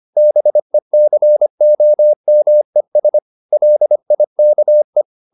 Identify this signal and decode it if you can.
Morse code, heard as a keyed tone, about 25 wpm, 600 Hz. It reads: BECOMES LIKE